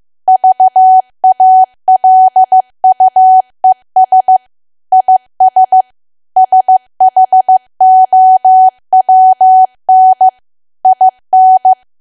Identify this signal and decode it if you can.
Morse code, heard as a keyed tone, about 15 wpm, 750 Hz. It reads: VALUES IS SHOWN IN
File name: cwa / W1AW-Level2 / 15WPM